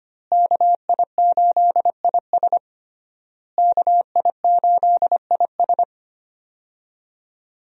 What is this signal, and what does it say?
Morse code, heard as a keyed tone, about 25 wpm, 700 Hz. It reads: KI8IS KI8IS
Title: USA Call Signs - 5 Characters - Repeated 3x - 25wpm